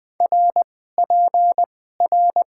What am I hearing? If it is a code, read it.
Morse code, heard as a keyed tone, about 20 wpm, spaced as for 14 wpm, 700 Hz. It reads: RPR
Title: Top 90 QSO Elements - 2x Character Spacing - 20wpm